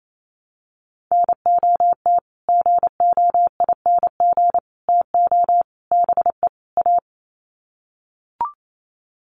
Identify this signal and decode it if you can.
Morse code, heard as a keyed tone, about 28 wpm, 700 Hz. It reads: NOT GOING TO BE A
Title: Sets of 5 Words Encore 28wpm